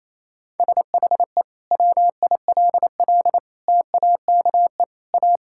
Morse code, heard as a keyed tone, about 28 wpm, 700 Hz. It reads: SHE WILL TAKE A
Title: Sentences from Top 1600 Words 28wpm